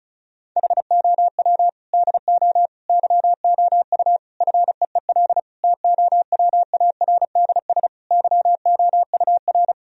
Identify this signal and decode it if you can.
Morse code, heard as a keyed tone, about 34 wpm, 700 Hz. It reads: HOW DO YOU FEEL TOWARDS YOUR
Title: Sentences from Top 1000 Words - Review Part 2 - 35wpm